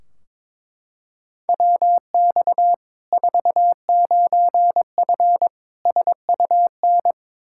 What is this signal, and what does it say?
Morse code, heard as a keyed tone, about 22 wpm, 700 Hz. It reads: WX 49F SUN
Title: Practice QSO - Exchange 2 - Info - 22wpm